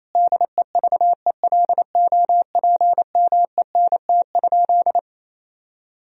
Morse code, heard as a keyed tone, about 28 wpm, 700 Hz. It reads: DEVELOPMENT?
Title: Sentences from Top 1000 Words - Review Part 4 - 28wpm